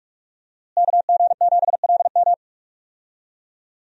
Morse code, heard as a keyed tone, about 45 wpm, 700 Hz. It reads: KG7LM
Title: USA Call Signs - 5 Characters - Repeated 2x - 45wpm